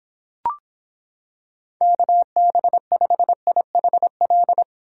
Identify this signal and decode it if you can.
Morse code, heard as a keyed tone, about 26 wpm, 700 Hz. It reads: KB5IHL